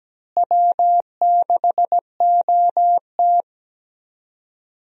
Morse code, heard as a keyed tone, about 17 wpm, 700 Hz. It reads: W6OT
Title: USA Call Signs - 4 Characters - Repeated 3x - 17wpm